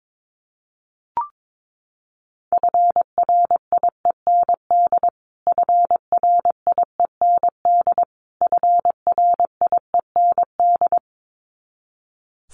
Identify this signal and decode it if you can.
Morse code, heard as a keyed tone, about 22 wpm, 700 Hz. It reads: FRIEND FRIEND FRIEND
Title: Top 500 Words Repeated 22wpm